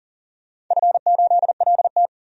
Morse code, heard as a keyed tone, about 40 wpm, 700 Hz. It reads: F8LT
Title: DXCC-100 Call Signs - 4 Characters - 40wpm